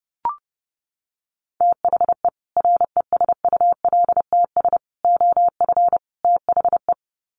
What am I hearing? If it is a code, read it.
Morse code, heard as a keyed tone, about 30 wpm, 700 Hz. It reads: THE RESULTS OF THE
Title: Sets of 4 Words 30wpm